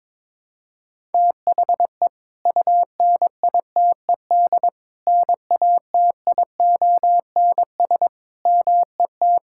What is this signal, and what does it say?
Morse code, heard as a keyed tone, about 22 wpm, 700 Hz. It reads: THE UNITED NATIONS MET
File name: practice / 22wpm / Sentences with 3-Word Phrases - 22wpm